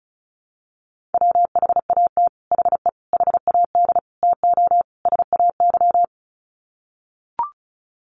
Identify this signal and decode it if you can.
Morse code, heard as a keyed tone, about 34 wpm, 700 Hz. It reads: WHAT HE HAD TO SAY
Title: Sets of 5 Words Encore2 35wpm